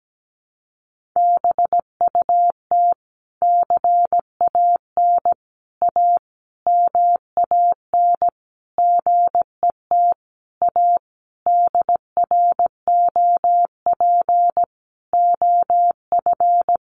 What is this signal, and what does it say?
Morse code, heard as a keyed tone, about 17 wpm, 700 Hz. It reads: BUT CAN A MAN GET A DROP OF
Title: Sentences from Top 1400 Words 17wpm